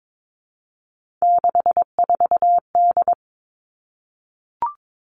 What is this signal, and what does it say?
Morse code, heard as a keyed tone, about 22 wpm, 700 Hz. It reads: 64D
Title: Three Challenging Letter-Number Combinations 22wpm